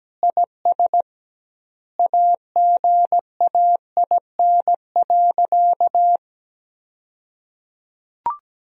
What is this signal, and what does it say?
Morse code, heard as a keyed tone, about 17 wpm, 700 Hz. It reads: IS AGAIN.
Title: 2x Word-Spacing - Sentences from Top 200 Words - 17wpm